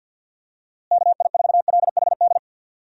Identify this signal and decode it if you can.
Morse code, heard as a keyed tone, about 51 wpm, 700 Hz. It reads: KI4LHD
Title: USA Call Signs - 6 Characters - Repeated 2x - 50wpm